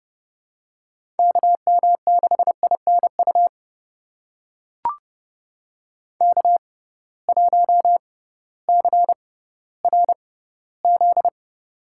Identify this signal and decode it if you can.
Morse code, heard as a keyed tone, about 30 wpm, spaced as for 16 wpm, 700 Hz. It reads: KM6INU K1CRZ
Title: USA Call Signs - 6x Character Spacing - 30wpm